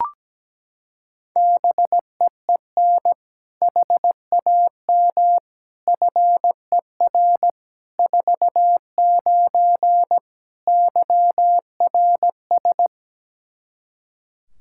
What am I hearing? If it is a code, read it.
Morse code, heard as a keyed tone, about 17 wpm, 700 Hz. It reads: BEEN HAM FER 49 YRS
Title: K1USN SST QSO Practice 17wpm